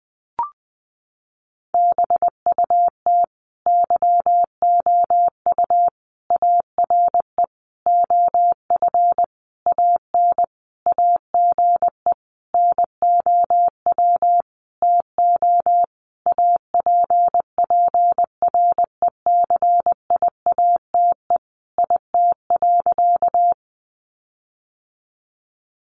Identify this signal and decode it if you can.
Morse code, heard as a keyed tone, about 20 wpm, 700 Hz. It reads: BUT YOU ARE OF AN AGE NOW TO APPRECIATE IT.
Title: Sentences from Top 1300 Words 20wpm